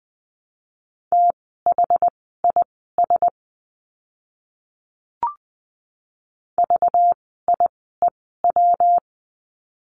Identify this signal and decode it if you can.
Morse code, heard as a keyed tone, about 20 wpm, spaced as for 15 wpm, 700 Hz. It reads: THIS VIEW
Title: 192 Lesson 31f - Letter v - Words 20wpm